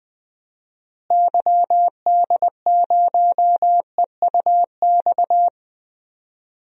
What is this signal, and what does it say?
Morse code, heard as a keyed tone, about 21 wpm, 700 Hz. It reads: YD0EUX
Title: DXCC-100 Call Signs - 6 Characters - Repeated 3x - 20wpm